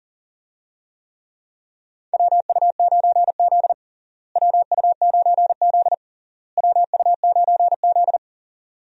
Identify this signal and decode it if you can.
Morse code, heard as a keyed tone, about 37 wpm, 700 Hz. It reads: WU9Z WU9Z WU9Z
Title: USA Call Signs - 4 Characters - Repeated 3x - 40wpm